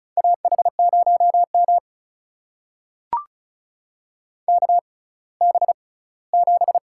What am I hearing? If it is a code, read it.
Morse code, heard as a keyed tone, about 34 wpm, spaced as for 20 wpm, 700 Hz. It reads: AH0M KB7
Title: USA Call Signs - 6x Character Spacing - 35wpm